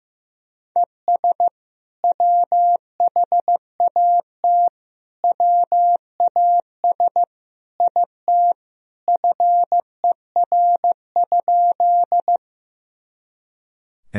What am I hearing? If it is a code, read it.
Morse code, heard as a keyed tone, about 15 wpm, 700 Hz. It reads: ES WHAT WAS IT FER?
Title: Sentences from Top 100 Words and Abbreviations 15wpm